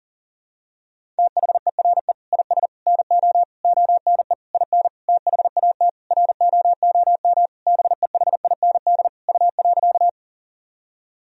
Morse code, heard as a keyed tone, about 40 wpm, 700 Hz. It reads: THERE IS NO ONE IN THAT ROOM BEHIND U.
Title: Sentences from Top 500 Words and Abbreviations 40wpm